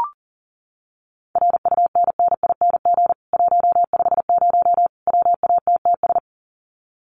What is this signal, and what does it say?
Morse code, heard as a keyed tone, about 40 wpm, 700 Hz. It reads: RUNNING 150 WATTS
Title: Common QSO Phrases 40wpm